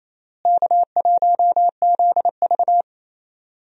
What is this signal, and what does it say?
Morse code, heard as a keyed tone, about 28 wpm, 700 Hz. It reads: K1ZV
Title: USA Call Signs - 4 Characters - Repeated 2x - 28wpm